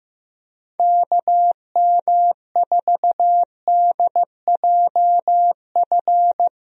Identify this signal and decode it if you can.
Morse code, heard as a keyed tone, about 15 wpm, 700 Hz. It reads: KM4DJF
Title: USA Call Signs - 6 Characters - Repeated 2x - 15wpm